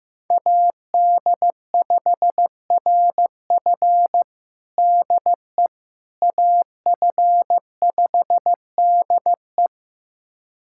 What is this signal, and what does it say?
Morse code, heard as a keyed tone, about 15 wpm, 700 Hz. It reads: AD5RF DE AF5DE
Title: USA Callsign DE Callsign - 5 Characters - 15wpm